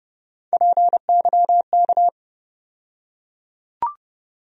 Morse code, heard as a keyed tone, about 30 wpm, 700 Hz. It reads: PYK
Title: USA Call Sign Postfixes - 3 Letters 30wpm